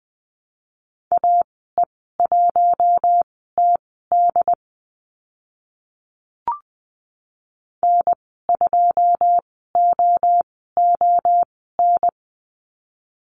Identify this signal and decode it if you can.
Morse code, heard as a keyed tone, about 20 wpm, spaced as for 14 wpm, 700 Hz. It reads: AE1TD N2OON
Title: 082 Lesson 14g - Number 2 - Call Signs 20wpm